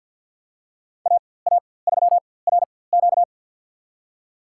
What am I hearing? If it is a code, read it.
Morse code, heard as a keyed tone, about 50 wpm, spaced as for 23 wpm, 700 Hz. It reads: AA3RQ
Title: USA Call Signs - 5 Characters - 4x Character Spacing - 50wpm